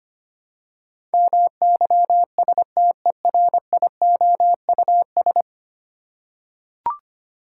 Morse code, heard as a keyed tone, about 25 wpm, 690 Hz. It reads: MYSTERIOUS
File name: 10 Letter Words - First 3 Letters Given - 25wpm